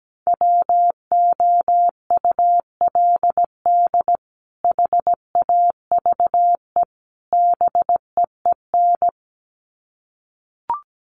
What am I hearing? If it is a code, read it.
Morse code, heard as a keyed tone, about 17 wpm, 700 Hz. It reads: WOULD HAVE BEEN
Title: Sets of 3 Words 17wpm